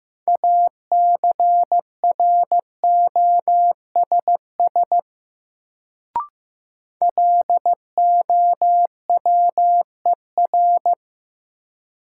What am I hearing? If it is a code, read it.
Morse code, heard as a keyed tone, about 15 wpm, 700 Hz. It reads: ACROSS LOWER
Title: Sentences from Top 700 Words - New Word Drill - 15wpm